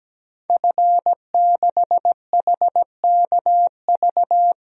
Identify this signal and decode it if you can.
Morse code, heard as a keyed tone, about 17 wpm, 700 Hz. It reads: F6HKV
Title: DXCC-100 Call Signs - 5 Characters - 17wpm